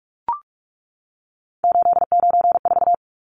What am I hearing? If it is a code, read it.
Morse code, heard as a keyed tone, about 44 wpm, 700 Hz. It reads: 894
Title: Numbers - 3-Digit 45wpm